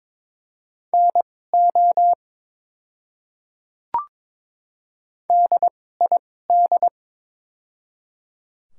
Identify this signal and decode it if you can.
Morse code, heard as a keyed tone, about 22 wpm, spaced as for 16 wpm, 700 Hz. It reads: NO DID
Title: 2x Character Spacing - Top 100 Words - 22wpm